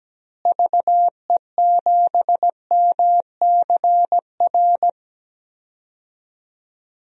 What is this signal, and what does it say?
Morse code, heard as a keyed tone, about 17 wpm, 700 Hz. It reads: VE7MCR
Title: DXCC-100 Call Signs - 6 Characters - Repeated 3x - 17wpm